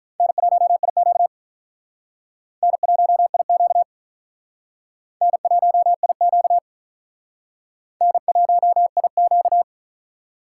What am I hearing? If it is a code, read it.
Morse code, heard as a keyed tone, about 43 wpm, 700 Hz. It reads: N1IQ N1IQ N1IQ N1IQ
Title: Speed Racing - USA Call Signs 35wpm